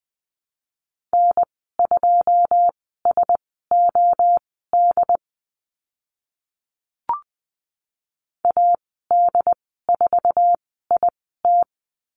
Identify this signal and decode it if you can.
Morse code, heard as a keyed tone, about 20 wpm, spaced as for 14 wpm, 700 Hz. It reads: N2SOD AD4IT
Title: 082 Lesson 14g - Number 2 - Call Signs 20wpm